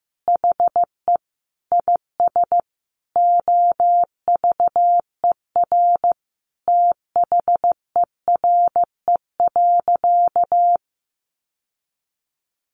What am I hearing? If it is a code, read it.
Morse code, heard as a keyed tone, about 15 wpm, 700 Hz. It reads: HE IS OVER THERE.
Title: Sentences from Top 600 Words 15wpm